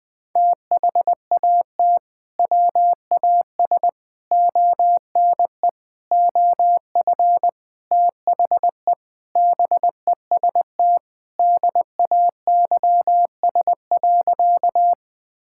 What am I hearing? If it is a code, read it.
Morse code, heard as a keyed tone, about 20 wpm, 700 Hz. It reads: THAT WAS ONE OF THE BEST DAYS.
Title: Sentences with 5-Word Phrases - 20wpm